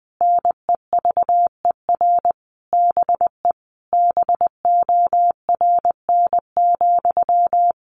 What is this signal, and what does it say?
Morse code, heard as a keyed tone, about 20 wpm, 700 Hz. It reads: NEVER BE BORN,
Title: A Sound Of Thunder 20wpm